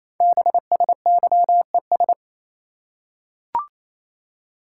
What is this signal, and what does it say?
Morse code, heard as a keyed tone, about 28 wpm, 700 Hz. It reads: BSYES
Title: Code Groups - 5 Letters - 28wpm